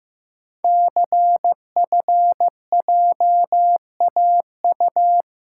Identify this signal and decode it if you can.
Morse code, heard as a keyed tone, about 15 wpm, 700 Hz. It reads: CFJAU
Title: Code Groups - 5 Letters - 15wpm